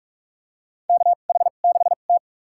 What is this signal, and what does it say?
Morse code, heard as a keyed tone, about 44 wpm, spaced as for 32 wpm, 700 Hz. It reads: KH6T
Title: USA Call Signs - 4 Characters - 2x Character Spacing - 45wpm